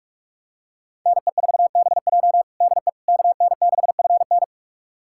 Morse code, heard as a keyed tone, about 43 wpm, 700 Hz. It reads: NE4BJ DE KN6FN
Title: USA Callsign DE Callsign - 5 Characters - 45wpm